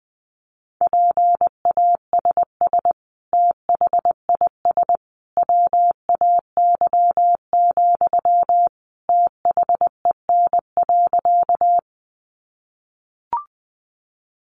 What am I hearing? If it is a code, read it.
Morse code, heard as a keyed tone, about 19 wpm, 700 Hz. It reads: PASS THIS WAY, THEN.
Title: Sentences from Top 400 Words 20wpm